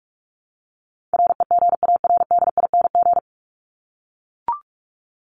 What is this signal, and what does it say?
Morse code, heard as a keyed tone, about 44 wpm, 700 Hz. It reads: REGARDING
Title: ING Suffix - Top 100 Words with Base Words 45wpm